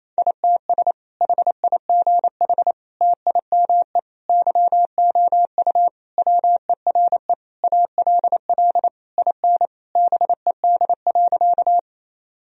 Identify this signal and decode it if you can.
Morse code, heard as a keyed tone, about 28 wpm, 700 Hz. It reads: ITS HIGH TIME YOU WERE ALL IN BED.
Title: Sentences from Top 1000 Words - Review Part 1 - 28wpm